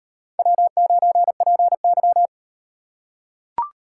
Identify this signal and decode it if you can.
Morse code, heard as a keyed tone, about 38 wpm, 700 Hz. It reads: W9PY